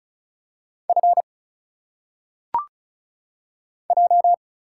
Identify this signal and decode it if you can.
Morse code, heard as a keyed tone, about 34 wpm, 700 Hz. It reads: F J
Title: Single Letters - Phonetic Pronunciation 35wpm